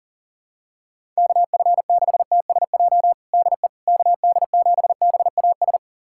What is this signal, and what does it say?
Morse code, heard as a keyed tone, about 40 wpm, 700 Hz. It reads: KF6TSJ DE KD7BAS
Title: USA Callsign DE Callsign - 6 Characters - 40wpm